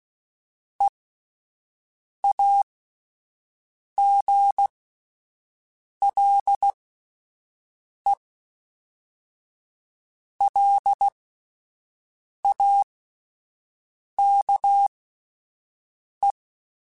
Morse code, heard as a keyed tone, about 16 wpm, spaced as for 5 wpm, 800 Hz. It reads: EAGLELAKE
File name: fromSMs / FromSMs mp3s / Cities@5